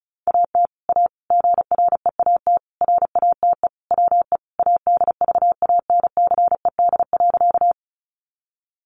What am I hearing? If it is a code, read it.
Morse code, heard as a keyed tone, about 34 wpm, 700 Hz. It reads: AT A GREAT RATE WE ADVANCED.
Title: Sentences from Top 1200 Words 35wpm